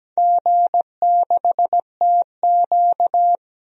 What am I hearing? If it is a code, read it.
Morse code, heard as a keyed tone, about 17 wpm, 700 Hz. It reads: G6TQ
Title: DXCC-100 Call Signs - 4 Characters - 17wpm